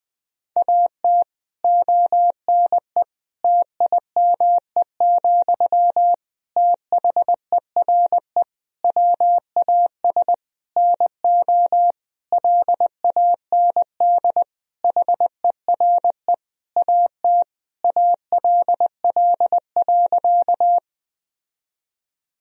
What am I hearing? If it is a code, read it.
Morse code, heard as a keyed tone, about 20 wpm, 700 Hz. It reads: AT ONE TIME, THERE WAS NO LAND HERE AT ALL.
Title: Sentences from Top 1000 Words - Review Part 1 - 20wpm